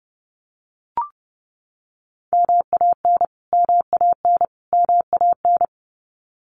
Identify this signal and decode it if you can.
Morse code, heard as a keyed tone, about 30 wpm, 700 Hz. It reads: MAN MAN MAN
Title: Top 500 Words Repeated 30wpm